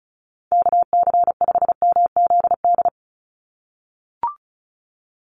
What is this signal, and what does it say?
Morse code, heard as a keyed tone, about 34 wpm, 700 Hz. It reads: KC5MZD
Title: USA Call Signs 35wpm